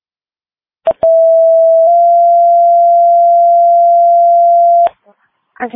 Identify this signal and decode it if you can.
Two-tone page, heard as a one-second tone, then a three-second tone, about 660 then 690 Hz, with speech. It is heard from 1.0 to 4.9 s.